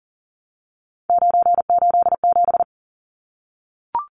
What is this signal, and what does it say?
Morse code, heard as a keyed tone, about 40 wpm, 700 Hz. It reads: 987
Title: Numbers - 3-Digit 40wpm